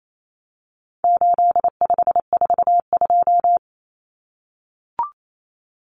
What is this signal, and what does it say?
Morse code, heard as a keyed tone, about 28 wpm, 700 Hz. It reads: 8542
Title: Common Cut Numbers - 4-Digit 28wpm